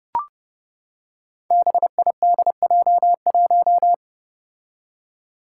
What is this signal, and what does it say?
Morse code, heard as a keyed tone, about 29 wpm, 700 Hz. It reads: BIDJ1
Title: Five Challenging Letter-Number Combinations 30wpm